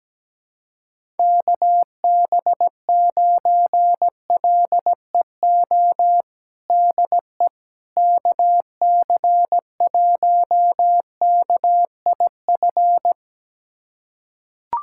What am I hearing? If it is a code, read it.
Morse code, heard as a keyed tone, about 17 wpm, 700 Hz. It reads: KB9LEO DE KC1KIF
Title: USA Callsign DE Callsign - 6 Characters - 17wpm